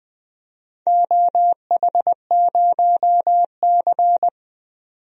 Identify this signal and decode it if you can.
Morse code, heard as a keyed tone, about 20 wpm, 700 Hz. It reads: OH0C
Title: DXCC-100 Call Signs - 4 Characters - 20wpm